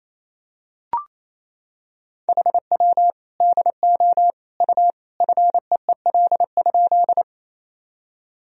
Morse code, heard as a keyed tone, about 27 wpm, 700 Hz. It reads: HW DO U FEEL?
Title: Sentences from Top 400 Words and Abbreviations 28wpm